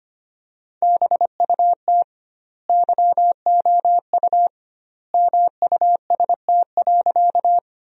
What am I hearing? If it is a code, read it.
Morse code, heard as a keyed tone, about 25 wpm, 700 Hz. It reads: BUT YOU MUST.
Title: 2x Word-Spacing - Sentences from Top 200 Words - 25wpm